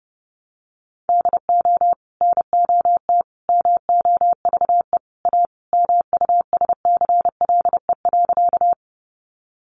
Morse code, heard as a keyed tone, about 30 wpm, 700 Hz. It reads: DO NOT MOVE A MUSCLE.
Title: Sentences from Top 1300 Words 30wpm